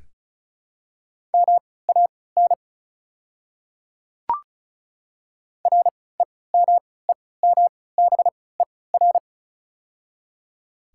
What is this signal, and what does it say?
Morse code, heard as a keyed tone, about 34 wpm, spaced as for 20 wpm, 700 Hz. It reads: MAN REMEMBER
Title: 3x Character Spacing - Top 500 Words - 35wpm